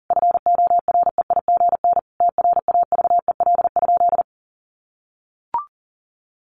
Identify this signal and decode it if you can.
Morse code, heard as a keyed tone, about 40 wpm, 700 Hz. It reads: FOREIGN TRAVEL?
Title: Sentences from Top 1400 Words 40wpm